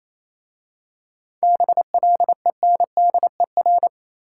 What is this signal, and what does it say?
Morse code, heard as a keyed tone, about 27 wpm, 690 Hz. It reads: BLENDER
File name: Easy Words - Speed-Racing with Hints - 22wpm